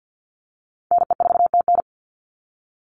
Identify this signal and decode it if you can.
Morse code, heard as a keyed tone, about 50 wpm, 700 Hz. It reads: NE4TN
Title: NA SOTA Chaser Callsigns 50wpm